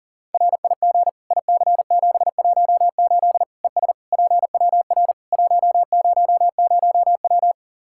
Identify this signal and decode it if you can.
Morse code, heard as a keyed tone, about 40 wpm, 700 Hz. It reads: RIG IC718 ES PWR 100W
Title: Practice QSO - Exchange 2 - Info - 40wpm